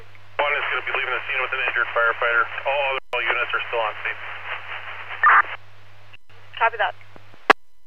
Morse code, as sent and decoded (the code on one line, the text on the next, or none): none